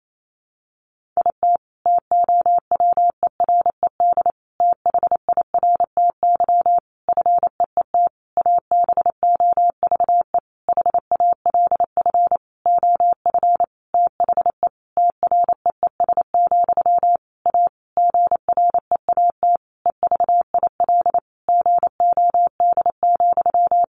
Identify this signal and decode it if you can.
Morse code, heard as a keyed tone, about 28 wpm, 700 Hz. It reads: IT TOWERED THIRTY FEET ABOVE HALF OF THE TREES, A GREAT EVIL GOD,
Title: A Sound Of Thunder 28wpm